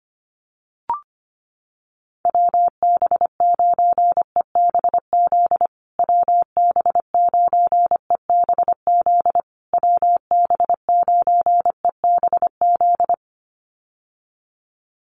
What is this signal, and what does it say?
Morse code, heard as a keyed tone, about 25 wpm, 700 Hz. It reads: WB9EBZ WB9EBZ WB9EBZ
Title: USA Call Signs Repeated 3x 25wpm